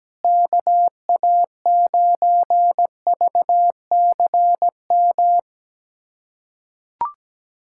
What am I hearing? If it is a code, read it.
Morse code, heard as a keyed tone, about 17 wpm, 700 Hz. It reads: KA9VCM
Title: USA Callsign DE Callsign 17wpm